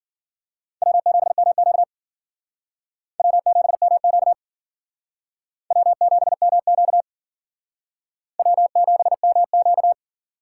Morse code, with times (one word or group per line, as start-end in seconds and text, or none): 0.8-1.8 W7MQ
3.2-4.3 W7MQ
5.7-7.0 W7MQ
8.4-9.9 W7MQ